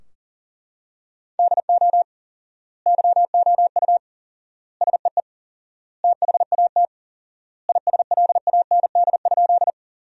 Morse code, heard as a keyed tone, about 40 wpm, 700 Hz. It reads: DO YOU SEE THAT ISLAND?
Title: 4x Word-Spacing - Sentences from Top 500 Words - 40wpm